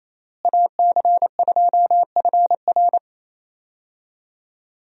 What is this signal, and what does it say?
Morse code, heard as a keyed tone, about 28 wpm, 700 Hz. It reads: AC2FR
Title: USA Call Signs - 5 Characters - Repeated 3x - 28wpm